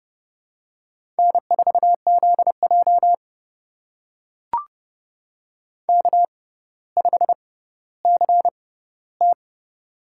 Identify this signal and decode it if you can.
Morse code, heard as a keyed tone, about 30 wpm, spaced as for 15 wpm, 690 Hz. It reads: N4ZJ K5CT
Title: USA Call Signs - 4 Characters - 6x Character Spacing - 30wpm